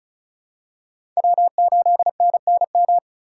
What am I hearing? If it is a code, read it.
Morse code, heard as a keyed tone, about 36 wpm, 700 Hz. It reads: W8NNM
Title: USA Call Signs - 5 Characters - 2x Character Spacing - 35wpm